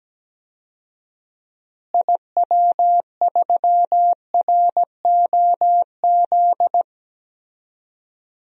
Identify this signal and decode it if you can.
Morse code, heard as a keyed tone, about 17 wpm, 700 Hz. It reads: IW3ROZ
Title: DXCC-100 Call Signs - 6 Characters - Repeated 2x - 17wpm